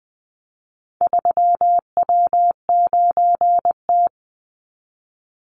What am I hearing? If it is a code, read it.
Morse code, heard as a keyed tone, about 20 wpm, 700 Hz. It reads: 3W9T
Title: Call Signs - 20wpm